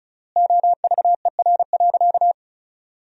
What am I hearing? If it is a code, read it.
Morse code, heard as a keyed tone, about 34 wpm, 700 Hz. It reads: OVER.
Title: 3x Word-Spacing - Sentences from Top 200 Words - 35wpm